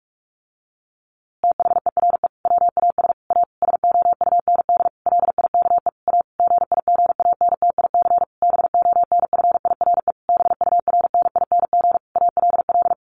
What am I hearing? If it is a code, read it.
Morse code, heard as a keyed tone, about 44 wpm, 700 Hz. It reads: THERE WAS A SOUND LIKE A GIGANTIC BONFIRE BURNING ALL
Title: A Sound Of Thunder 45wpm